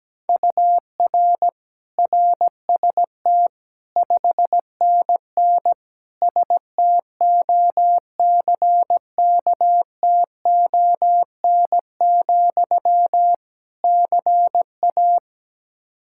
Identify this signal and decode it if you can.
Morse code, heard as a keyed tone, about 17 wpm, 700 Hz. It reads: UR RST 5NN STOCKTON, CA
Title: K1USN SST QSO Practice 17wpm